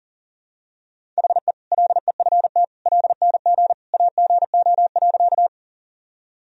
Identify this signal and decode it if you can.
Morse code, heard as a keyed tone, about 40 wpm, 700 Hz. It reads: HE LEFT LNG AGO.
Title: Sentences from Top 500 Words and Abbreviations 40wpm